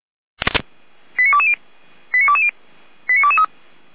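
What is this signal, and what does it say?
Five-tone selective call (ZVEI), heard as quick runs of five tones, tones of about 70 ms. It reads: 89220 89220 89223
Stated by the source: Five-tone ZVEI selective call received on an SDR.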